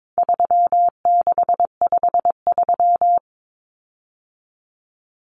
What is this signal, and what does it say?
Morse code, heard as a keyed tone, about 22 wpm, 700 Hz. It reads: 3653
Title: Common Cut Numbers - 4-Digit 22wpm